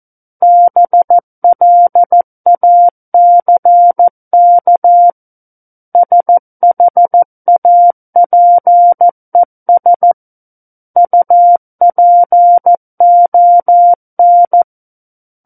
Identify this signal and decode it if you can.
Morse code, heard as a keyed tone, about 14 wpm, 700 Hz. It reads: BLACK SHAPES UPON
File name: Book One / 05-Chapter 5 - 14 WPM